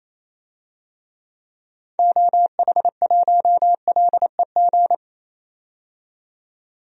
Morse code, heard as a keyed tone, about 28 wpm, 700 Hz. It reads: OH1LEG
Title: DXCC-100 Call Signs - 6 Characters - 28wpm